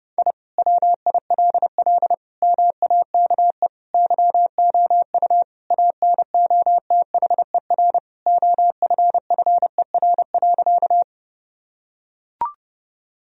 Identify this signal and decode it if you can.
Morse code, heard as a keyed tone, about 32 wpm, 700 Hz. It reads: I WILL MAKE YOU ANOTHER OFFER.
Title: Sentences from Top 1000 Words - Review Part 4 - 30wpm